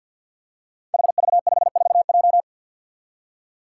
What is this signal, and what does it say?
Morse code, heard as a keyed tone, about 51 wpm, 700 Hz. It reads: HV54J
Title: Five Challenging Letter-Number Combinations 50wpm